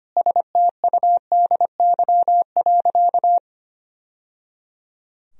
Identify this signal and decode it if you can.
Morse code, heard as a keyed tone, about 25 wpm, 700 Hz. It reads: STUDY.
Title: Sentences from Top 1000 Words - Review Part 1 - 25wpm